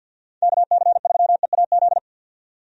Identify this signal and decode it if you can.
Morse code, heard as a keyed tone, about 48 wpm, 700 Hz. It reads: KC3EAZ
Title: USA Call Signs - 6 Characters - 50wpm